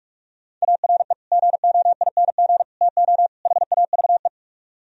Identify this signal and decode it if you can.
Morse code, heard as a keyed tone, about 47 wpm, 700 Hz. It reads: ARE GOING TO HAVE